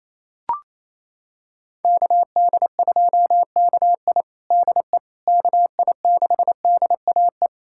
Morse code, heard as a keyed tone, about 27 wpm, 700 Hz. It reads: KD2KI DE KI6DAE
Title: USA Callsign DE Callsign 28wpm